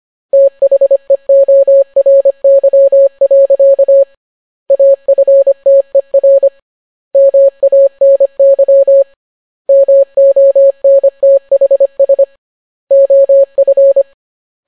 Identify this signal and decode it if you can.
Morse code, heard as a keyed tone, about 25 wpm, 550 Hz. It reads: THEORY. AFTER MANY MONTHS OF